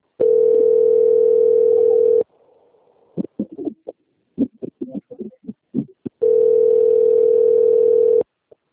Ringback tone (North America), 440 + 480 Hz.
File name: out- - - - - .14903